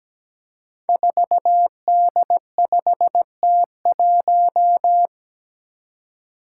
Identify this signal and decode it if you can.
Morse code, heard as a keyed tone, about 17 wpm, 700 Hz. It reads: 4D5T1
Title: Five Challenging Letter-Number Combinations 17wpm